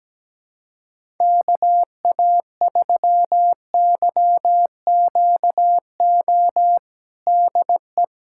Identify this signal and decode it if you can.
Morse code, heard as a keyed tone, about 17 wpm, 700 Hz. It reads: KA3YQO DE
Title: USA Callsign DE Callsign - 6 Characters - 17wpm